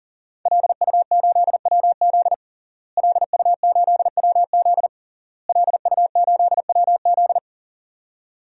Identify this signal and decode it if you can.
Morse code, heard as a keyed tone, about 40 wpm, 700 Hz. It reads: LU8WZ LU8WZ LU8WZ
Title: DXCC-100 Call Signs - 5 Characters - Repeated 3x - 40wpm